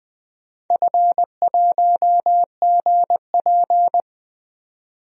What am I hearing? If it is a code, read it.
Morse code, heard as a keyed tone, about 20 wpm, 700 Hz. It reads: F1GP
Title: DXCC-100 Call Signs - 4 Characters - 20wpm